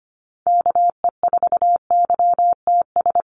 Morse code, heard as a keyed tone, about 25 wpm, 700 Hz. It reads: KE4YTS
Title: USA Call Signs Repeated 3x 25wpm